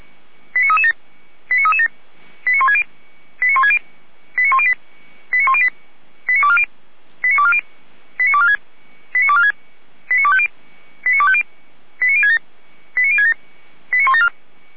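Five-tone selective call (ZVEI), heard as quick runs of five tones, tones of about 70 ms. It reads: 89337 89337 89160 89160 89108 89108 89240 89240 89246 89246 89250 89250 89067 89067
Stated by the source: Five-tone ZVEI selective call 89075 received by SDR.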